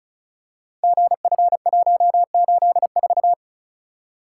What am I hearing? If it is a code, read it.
Morse code, heard as a keyed tone, about 36 wpm, 700 Hz. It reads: GF184
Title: Code Groups - 5 Letter:Numbers - 35wpm